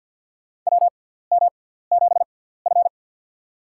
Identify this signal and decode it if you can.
Morse code, heard as a keyed tone, about 50 wpm, spaced as for 17 wpm, 700 Hz. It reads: WM7F
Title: USA Call Signs - 4 Characters - 6x Character Spacing - 50wpm